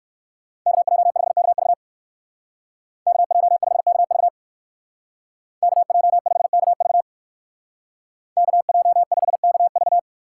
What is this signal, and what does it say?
Morse code, heard as a keyed tone, about 59 wpm, 700 Hz. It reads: KJ5KV KJ5KV KJ5KV KJ5KV